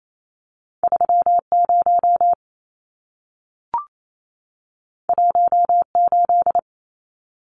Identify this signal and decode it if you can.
Morse code, heard as a keyed tone, about 28 wpm, 700 Hz. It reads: 30 18
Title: Numbers - 2-Digit 28wpm